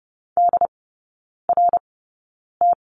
Morse code, heard as a keyed tone, about 30 wpm, spaced as for 9 wpm, 700 Hz. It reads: DRT
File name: 081 Lesson 14e - Number 2 - Three Characters 30wpm